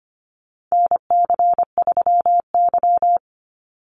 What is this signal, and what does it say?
Morse code, heard as a keyed tone, about 25 wpm, 700 Hz. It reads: NC3Y
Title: Call Signs - 25wpm